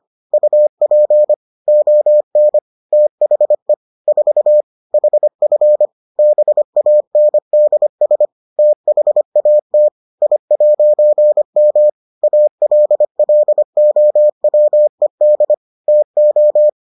Morse code, heard as a keyed tone, about 25 wpm, 600 Hz. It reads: UP ON THE 4 HF BANDS THAT I'M ALLOWED TO